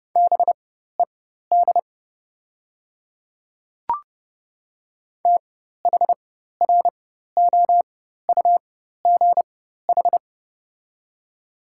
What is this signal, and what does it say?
Morse code, heard as a keyed tone, about 30 wpm, spaced as for 14 wpm, 700 Hz. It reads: BED THROUGH
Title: 4x Character Spacing - Top 500 Words - 30wpm